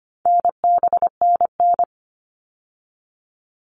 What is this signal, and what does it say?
Morse code, heard as a keyed tone, about 25 wpm, 700 Hz. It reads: NBNN
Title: All Cut Numbers - 4-Digit 25wpm